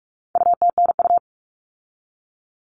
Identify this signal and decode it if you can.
Morse code, heard as a keyed tone, about 44 wpm, 700 Hz. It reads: UTNU
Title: All Cut Numbers - 4-Digit 45wpm